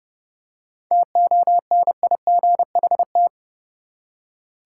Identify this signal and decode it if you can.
Morse code, heard as a keyed tone, about 30 wpm, 700 Hz. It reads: TONIGHT
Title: Seven Letter Words - 30wpm